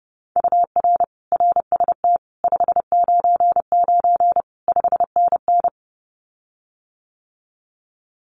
Morse code, heard as a keyed tone, about 30 wpm, 700 Hz. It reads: UR RST 599 5NN
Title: Common QSO Phrases Encore 30wpm